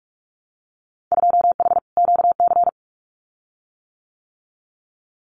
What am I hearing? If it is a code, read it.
Morse code, heard as a keyed tone, about 46 wpm, 700 Hz. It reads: 2H QC
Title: Winter Field Day Exchanges Encore 45wpm